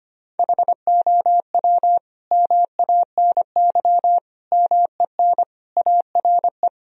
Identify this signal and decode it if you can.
Morse code, heard as a keyed tone, about 25 wpm, 700 Hz. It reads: HOW MANY MEN ARE
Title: Sentences from Top 1000 Words - Review Part 3 - 25wpm